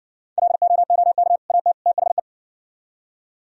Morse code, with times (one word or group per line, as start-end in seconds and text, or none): 0.4-1.4 LOOK
1.5-1.7 AT
1.9-2.2 THE